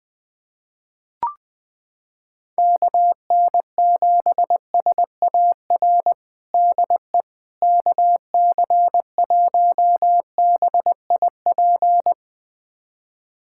Morse code, heard as a keyed tone, about 20 wpm, 700 Hz. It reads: KN7SAR DE KC1BIP
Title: USA Callsign DE Callsign 20wpm